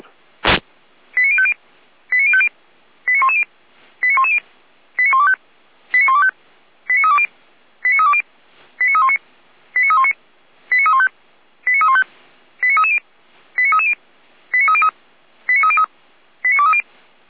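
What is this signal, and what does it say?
Five-tone selective call (ZVEI), heard as quick runs of five tones, tones of about 70 ms. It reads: 89050 89050 89110 89110 89125 89125 89320 89320 89319 89319 89315 89315 89330 89330 89303 89303 89230
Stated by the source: ZVEI five-tone selective call received by SDR, logged by Feuerwehr Orlamünde.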